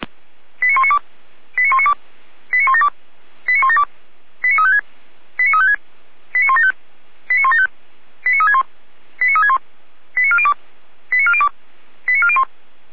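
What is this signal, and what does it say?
Five-tone selective call (ZVEI), heard as quick runs of five tones, tones of about 70 ms. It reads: 89192 89192 89172 89172 89356 89356 89175 89175 89361 89361 89402 89402 89401
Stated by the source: ZVEI five-tone selective call 89338 received on SDR.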